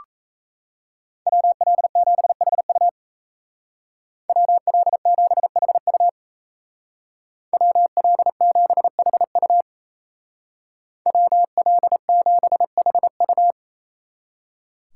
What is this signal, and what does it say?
Morse code, heard as a keyed tone, about 33 wpm, 700 Hz. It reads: WL7HU WL7HU WL7HU WL7HU
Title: Speed Racing - USA Call Signs 28wpm